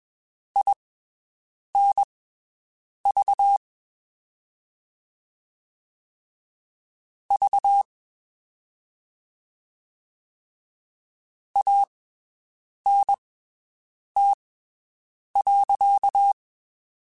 Morse code, heard as a keyed tone, about 21 wpm, spaced as for 7 wpm, 800 Hz. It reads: INV V ANT.
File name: (spaced) / QSOs Spaced-mp3s / WD0FBT 1 @20 sp